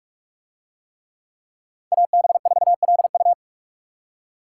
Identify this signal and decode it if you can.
Morse code, heard as a keyed tone, about 45 wpm, 700 Hz. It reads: AB4LU